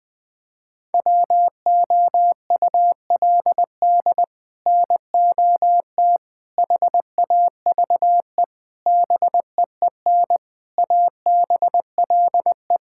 Morse code, heard as a keyed tone, about 20 wpm, 700 Hz. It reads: WOULD NOT HAVE BEEN ABLE
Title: Sets of 6 Words - 20wpm